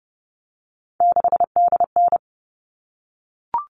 Morse code, heard as a keyed tone, about 30 wpm, 700 Hz. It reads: 6DN
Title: All Cut Numbers - 3-Digit 30wpm